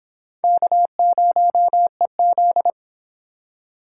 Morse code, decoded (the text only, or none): K0EZ